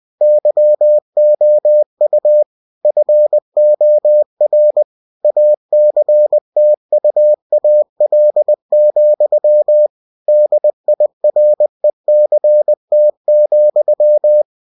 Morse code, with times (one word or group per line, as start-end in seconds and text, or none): 0.2-2.4 YOU
2.8-4.8 FOR
5.2-9.9 ACTUAL,
10.3-14.4 DIRECT,